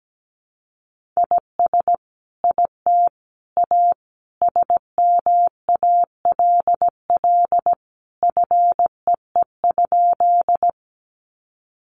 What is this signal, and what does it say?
Morse code, heard as a keyed tone, about 17 wpm, 700 Hz. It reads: IS IT A SMALL FEE?
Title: Sentences from Top 1500 Words 17wpm